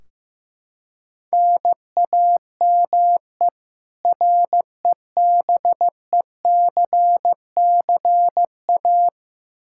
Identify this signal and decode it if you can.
Morse code, heard as a keyed tone, about 15 wpm, 700 Hz. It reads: NAME REBECCA
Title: K1USN SST QSO Practice 15wpm